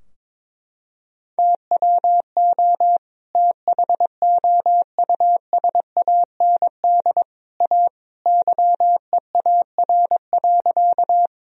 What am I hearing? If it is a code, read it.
Morse code, heard as a keyed tone, about 22 wpm, 700 Hz. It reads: TWO THOUSAND A YEAR.
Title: Sentences from Top 1000 Words - Review Part 3 - 22wpm